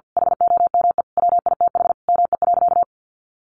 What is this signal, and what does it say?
Morse code, heard as a keyed tone, about 50 wpm, 700 Hz. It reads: HOME WITH ME.
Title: Sentences from Top 800 Words 50wpm